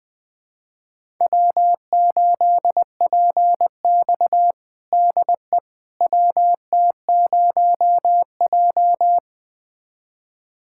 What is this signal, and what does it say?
Morse code, heard as a keyed tone, about 20 wpm, 700 Hz. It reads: W8PX DE WT0J
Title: USA Callsign DE Callsign - 4 Characters - 20wpm